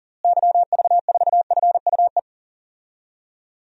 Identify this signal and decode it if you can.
Morse code, heard as a keyed tone, about 40 wpm, 700 Hz. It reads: YV4FUE
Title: DXCC-100 Call Signs - 6 Characters - Repeated 2x - 40wpm